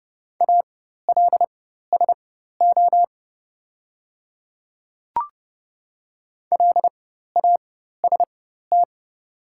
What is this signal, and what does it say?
Morse code, heard as a keyed tone, about 29 wpm, spaced as for 14 wpm, 700 Hz. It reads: ALSO LAST
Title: 4x Character Spacing - Top 300 Words - 30wpm